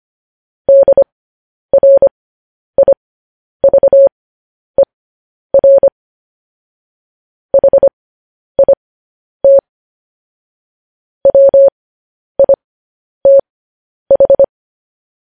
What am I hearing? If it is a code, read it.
Morse code, heard as a keyed tone, about 25 wpm, spaced as for 10 wpm, 550 Hz. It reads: DRIVER HIT WITH
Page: news headlines in morse code at 10 wpm